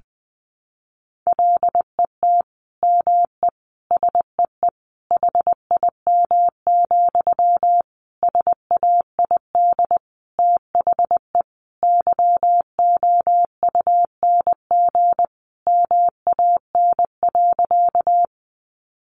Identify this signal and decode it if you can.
Morse code, heard as a keyed tone, about 20 wpm, 700 Hz. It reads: LET ME SEE HIM, SAID THE YOUNG MAN.
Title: Sentences from Top 400 Words 20wpm